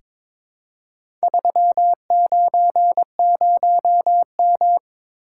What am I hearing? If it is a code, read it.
Morse code, heard as a keyed tone, about 22 wpm, 700 Hz. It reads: 390M
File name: practice / 22wpm / Four Letter-Number Combinations - 22wpm